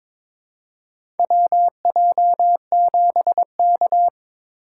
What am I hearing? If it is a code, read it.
Morse code, heard as a keyed tone, about 22 wpm, 700 Hz. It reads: WJ7K
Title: USA Call Signs - 2x Character Spacing - 22wpm